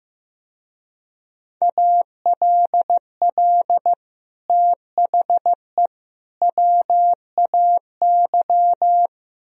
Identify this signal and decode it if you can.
Morse code, heard as a keyed tone, about 16 wpm, 700 Hz. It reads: ALL THE WAY
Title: Speed-Racing - Sets of 3 Words 15wpm